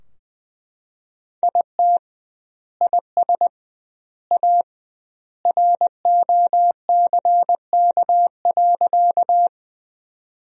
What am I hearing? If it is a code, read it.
Morse code, heard as a keyed tone, about 20 wpm, 700 Hz. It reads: IT IS A ROCK.
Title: 2x Word-Spacing - Sentences from Top 400 Words - 20wpm